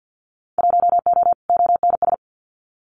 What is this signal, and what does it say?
Morse code, heard as a keyed tone, about 50 wpm, 700 Hz. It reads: 1O ONS
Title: Winter Field Day Exchanges Encore 50wpm